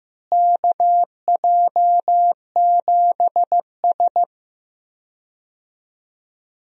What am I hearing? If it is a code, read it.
Morse code, heard as a keyed tone, about 15 wpm, 700 Hz. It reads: KJ7S
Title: USA Call Signs - 4 Characters - Repeated 2x - 15wpm